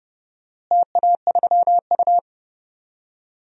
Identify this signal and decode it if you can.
Morse code, heard as a keyed tone, about 31 wpm, 700 Hz. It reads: TA3U
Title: DXCC-100 Call Signs - 4 Characters - Repeated 2x - 30wpm